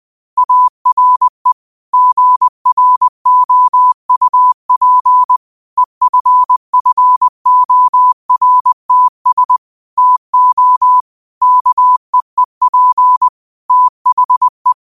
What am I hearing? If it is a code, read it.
Morse code, heard as a keyed tone, about 20 wpm, 1000 Hz. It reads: ARE GROUP EFFORTS TO KEEP THE
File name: qotdfiles / 20wpm